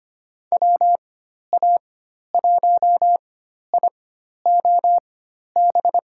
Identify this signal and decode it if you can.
Morse code, heard as a keyed tone, about 25 wpm, spaced as for 12 wpm, 700 Hz. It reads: WA1IOB
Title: USA Call Signs - 6 Characters - 4x Character Spacing - 25wpm